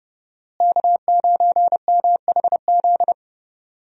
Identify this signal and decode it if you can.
Morse code, heard as a keyed tone, about 32 wpm, 700 Hz. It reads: K9MHZ
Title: USA Call Signs - 5 Characters - Repeated 2x - 30wpm